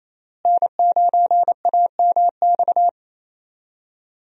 Morse code, heard as a keyed tone, about 28 wpm, 700 Hz. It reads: N9AMX
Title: USA Call Signs - 5 Characters - Repeated 3x - 28wpm